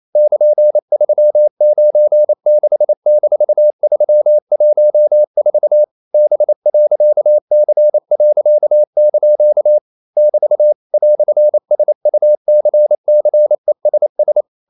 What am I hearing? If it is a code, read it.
Morse code, heard as a keyed tone, about 28 wpm, 600 Hz. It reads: <KN>396-314 B.C.) <BT> "SUCCESS
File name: CW / Practice / EnglishQuotations0228